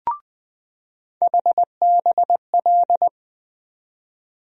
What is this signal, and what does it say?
Morse code, heard as a keyed tone, about 20 wpm, 700 Hz. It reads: HBL
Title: USA Call Sign Postfixes - 3 Letters 20wpm